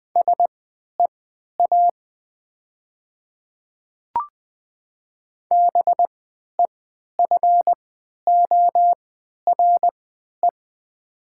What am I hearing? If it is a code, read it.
Morse code, heard as a keyed tone, about 20 wpm, spaced as for 11 wpm, 700 Hz. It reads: SEA BEFORE
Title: 3x Character Spacing - Top 500 Words - 20wpm